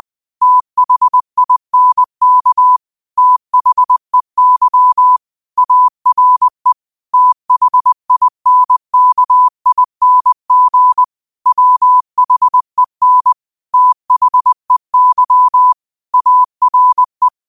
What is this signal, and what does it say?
Morse code, heard as a keyed tone, about 20 wpm, 1000 Hz. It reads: THINK THEY ARE THINKING WHEN THEY ARE